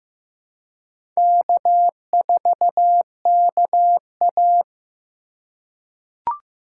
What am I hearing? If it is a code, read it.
Morse code, heard as a keyed tone, about 15 wpm, 700 Hz. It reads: K4KA
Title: USA Call Signs - 4 Characters - 2x Character Spacing - 15wpm